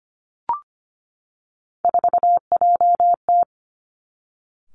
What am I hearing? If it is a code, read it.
Morse code, heard as a keyed tone, about 25 wpm, 700 Hz. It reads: 4JT
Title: Three Challenging Letter-Number Combinations 25wpm